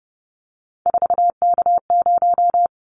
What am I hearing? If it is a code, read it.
Morse code, heard as a keyed tone, about 30 wpm, 700 Hz. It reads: 4K0
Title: Callsign Prefixes for Top 100 DXCC - 30wpm